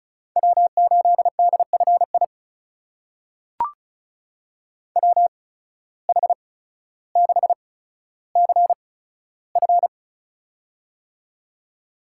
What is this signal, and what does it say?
Morse code, heard as a keyed tone, about 36 wpm, spaced as for 15 wpm, 700 Hz. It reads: W8DFI WH6CF
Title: USA Call Signs - 8x Character Spacing - 35wpm